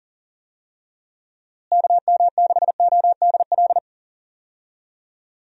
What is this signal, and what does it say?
Morse code, heard as a keyed tone, about 40 wpm, 700 Hz. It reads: KM6ODL
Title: USA Call Signs - 6 Characters - Repeated 2x - 40wpm